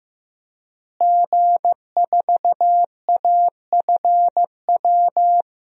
Morse code, heard as a keyed tone, about 15 wpm, 700 Hz. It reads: G4AFW
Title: DXCC-100 Call Signs - 5 Characters - Repeated 3x - 15wpm